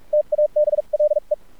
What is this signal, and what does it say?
Morse code, heard as a keyed tone, about 46 wpm, spaced as for 38 wpm, 600 Hz, with radio noise. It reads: TABLE
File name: archived / CRSnet